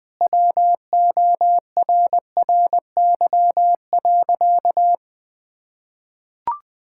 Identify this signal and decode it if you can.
Morse code, heard as a keyed tone, about 20 wpm, 700 Hz. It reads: WORRY.
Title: Sentences with 5-Word Phrases - Spoken Beginning - 20wpm